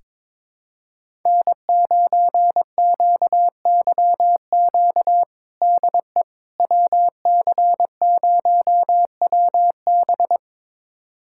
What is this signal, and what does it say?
Morse code, heard as a keyed tone, about 22 wpm, 700 Hz. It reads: N9QYQ DE WC0WB
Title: USA Callsign DE Callsign - 5 Characters - 22wpm